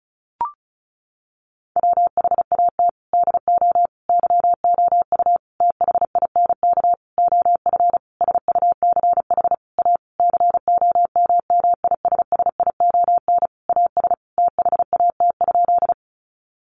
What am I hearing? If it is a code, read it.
Morse code, heard as a keyed tone, about 34 wpm, 700 Hz. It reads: WHAT DO YOU THINK OF SUCH A COMMISSION AS THAT?
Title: Sentences from Top 1400 Words 35wpm